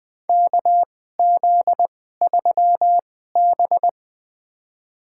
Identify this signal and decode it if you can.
Morse code, heard as a keyed tone, about 20 wpm, spaced as for 14 wpm, 700 Hz. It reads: KZ3B
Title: USA Call Signs - 4 Characters - 2x Character Spacing - 20wpm